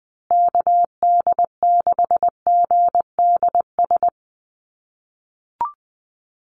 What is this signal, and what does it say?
Morse code, heard as a keyed tone, about 20 wpm, 700 Hz. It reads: KD6GDS
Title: USA Call Signs Repeated 3x 20wpm